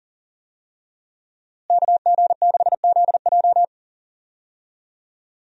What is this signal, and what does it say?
Morse code, heard as a keyed tone, about 37 wpm, 700 Hz. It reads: KG6ZJ
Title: USA Call Signs - 5 Characters - Repeated 3x - 40wpm